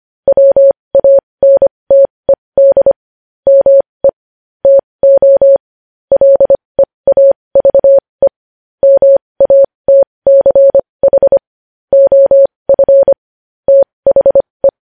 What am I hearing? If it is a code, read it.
Morse code, heard as a keyed tone, about 25 wpm, spaced as for 20 wpm, 550 Hz. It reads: WANTED ME TO LEAVE MATCH OF THE
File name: podcast / News headlines-1x-25x20